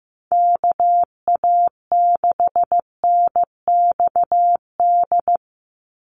Morse code, heard as a keyed tone, about 15 wpm, 700 Hz. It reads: KA6NXD
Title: USA Call Signs 15wpm